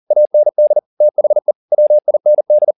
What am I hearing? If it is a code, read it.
Morse code, heard as a keyed tone, about 40 wpm, 600 Hz. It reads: AND THE WIND
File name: the old man and the sea 2 40wpm